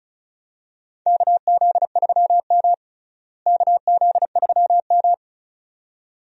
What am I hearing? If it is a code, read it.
Morse code, heard as a keyed tone, about 36 wpm, 700 Hz. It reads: KZ3M KZ3M
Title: USA Call Signs - 4 Characters - Repeated 2x - 35wpm